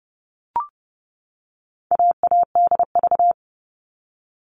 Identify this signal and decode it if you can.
Morse code, heard as a keyed tone, about 30 wpm, 700 Hz. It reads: AADV
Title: All Cut Numbers - 4-Digit 30wpm